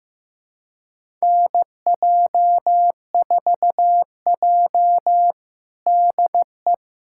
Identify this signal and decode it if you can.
Morse code, heard as a keyed tone, about 16 wpm, 700 Hz. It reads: NJ4J DE
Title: USA Callsign DE Callsign - 4 Characters - 15wpm